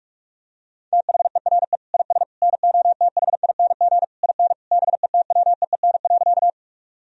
Morse code, heard as a keyed tone, about 46 wpm, 700 Hz. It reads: THERE IS NOTHING IN BETWEEN.
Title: Sentences from Top 1000 Words - Review Part 1 - 45wpm